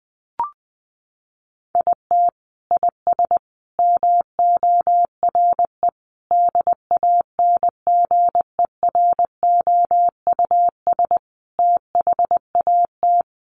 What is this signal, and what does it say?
Morse code, heard as a keyed tone, about 20 wpm, 700 Hz. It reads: IT IS MORE DANGEROUS THAT
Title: Sentences from Top 1200 Words 20wpm